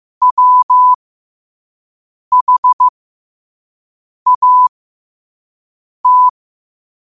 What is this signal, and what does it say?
Morse code, heard as a keyed tone, about 15 wpm, spaced as for 5 wpm, 1000 Hz. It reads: WHAT
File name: qotdfiles / 5wpm